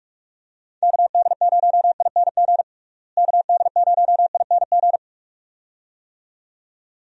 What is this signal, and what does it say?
Morse code, heard as a keyed tone, about 45 wpm, 700 Hz. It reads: KD0ING KD0ING
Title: USA Call Signs - 6 Characters - Repeated 2x - 45wpm